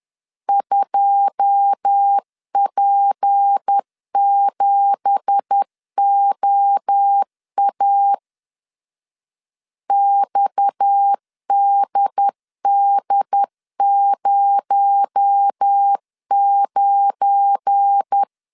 Morse code, heard as a keyed tone, about 10 wpm, 800 Hz. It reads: 2P7OA XDD09
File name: CW / DET / 1 9 1500C S B 10wpm 4439